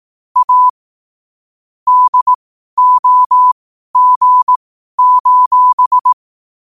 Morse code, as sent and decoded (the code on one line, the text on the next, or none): .- / -.. --- --. ---...
A DOG: